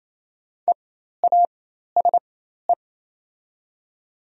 Morse code, heard as a keyed tone, about 28 wpm, spaced as for 13 wpm, 700 Hz. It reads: EASE
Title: 4x Character Spacing - Top 500 Words - 28wpm